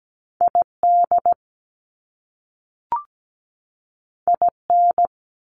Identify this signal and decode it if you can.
Morse code, heard as a keyed tone, about 17 wpm, 700 Hz. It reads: ID IN
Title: US States 17wpm